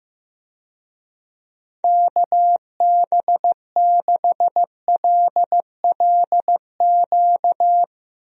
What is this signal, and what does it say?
Morse code, heard as a keyed tone, about 16 wpm, 700 Hz. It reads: KB6LLQ
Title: USA Call Signs - 6 Characters - Repeated 3x - 15wpm